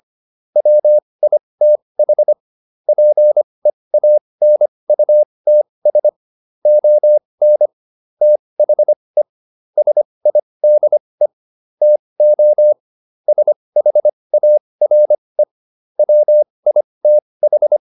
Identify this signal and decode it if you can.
Morse code, heard as a keyed tone, about 25 wpm, spaced as for 20 wpm, 600 Hz. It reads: WITH PEANUTS ON THE SIDE TO SHARE WITH